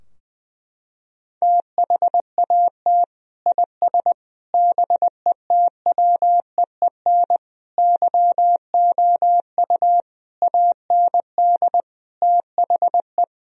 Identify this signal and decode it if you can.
Morse code, heard as a keyed tone, about 20 wpm, 700 Hz. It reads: THAT IS BETWEEN YOU AND THE
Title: Sentences from Top 1000 Words - Review Part 4 - 20wpm